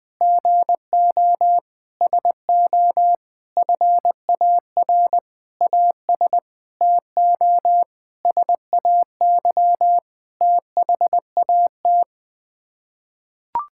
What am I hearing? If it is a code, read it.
Morse code, heard as a keyed tone, about 20 wpm, 700 Hz. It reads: GO SO FAR AS TO SAY THAT
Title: Sets of 9 Words - 20wpm